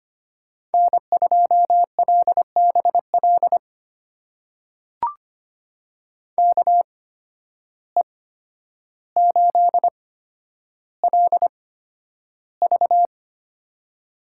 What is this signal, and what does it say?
Morse code, heard as a keyed tone, about 25 wpm, spaced as for 11 wpm, 700 Hz. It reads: N2LBL KE8LV
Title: USA Call Signs - 5 Characters - 8x Character Spacing - 25wpm